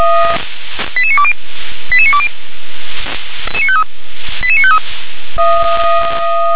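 Five-tone selective call (ZVEI), heard as quick runs of five tones, tones of about 70 ms. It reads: 88920 88920 88953
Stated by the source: ZVEI five-tone selective call received on an SDR.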